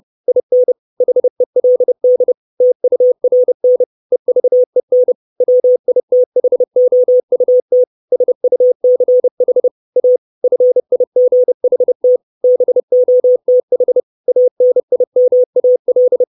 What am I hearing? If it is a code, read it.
Morse code, heard as a keyed tone, about 30 wpm, 500 Hz. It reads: IN HELD TURN EVEN WITHOUT SUCH A FIGHT BOTH ANIMAL
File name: 632 Words 30WPM